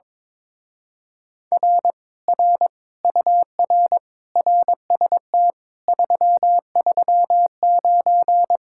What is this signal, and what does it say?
Morse code, heard as a keyed tone, about 22 wpm, 700 Hz. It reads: R R UR RST 339
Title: Speed-Racing Common QSO Phrases Encore - 22wpm